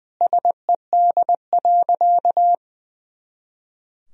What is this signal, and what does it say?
Morse code, heard as a keyed tone, about 20 wpm, 700 Hz. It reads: SED.
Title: Sentences from Top 300 Words and Abbreviations 20wpm